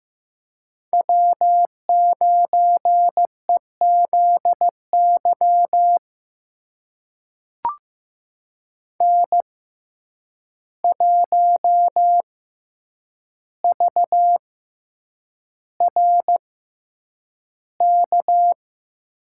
Morse code, heard as a keyed tone, about 15 wpm, spaced as for 8 wpm, 700 Hz. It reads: W9EZY N1VRK
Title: USA Call Signs - 5 Characters - 6x Character Spacing - 15wpm